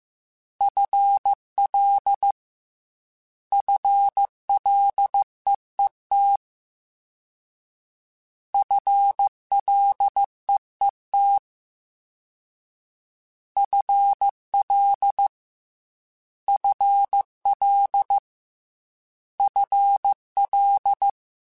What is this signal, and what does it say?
Morse code, heard as a keyed tone, about 15 wpm, 800 Hz. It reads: FL FLEET FLEET FL FL FL
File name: FL / FL mp3s / FL Prefixes 3x2 @13